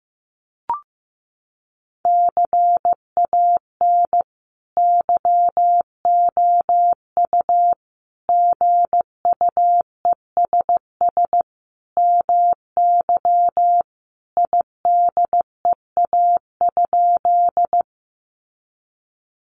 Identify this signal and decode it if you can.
Morse code, heard as a keyed tone, about 14 wpm, 700 Hz. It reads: CAN YOU GUESS MY IDEA?
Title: Sentences from Top 1500 Words 15wpm